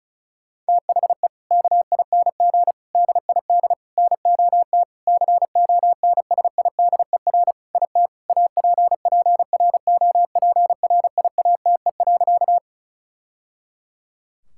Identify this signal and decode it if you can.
Morse code, heard as a keyed tone, about 36 wpm, 700 Hz. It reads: THE KING DID NOT CONSIDER IT APPROPRIATE.
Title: Sentences from Top 1000 Words - Review Part 5 - 35wpm